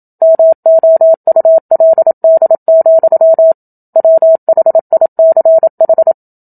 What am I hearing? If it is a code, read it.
Morse code, heard as a keyed tone, about 27 wpm, 650 Hz. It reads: MOULD, WHICH
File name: Book Two / 01-Chapter 1 - 27 WPM